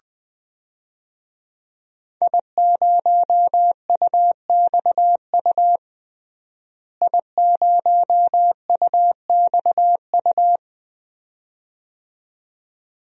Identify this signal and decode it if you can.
Morse code, heard as a keyed tone, about 21 wpm, 700 Hz. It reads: I0UXU I0UXU
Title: DXCC-100 Call Signs - 5 Characters - Repeated 2x - 20wpm